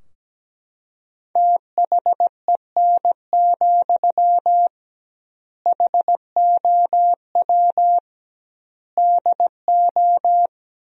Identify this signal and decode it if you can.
Morse code, heard as a keyed tone, about 17 wpm, 700 Hz. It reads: THEN, HOW DO
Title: 2x Word-Spacing - Sentences from Top 100 Words - 17wpm